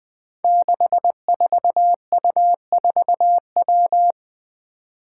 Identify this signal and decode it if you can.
Morse code, heard as a keyed tone, about 20 wpm, 700 Hz. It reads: 64U4W
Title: Five Challenging Letter-Number Combinations 20wpm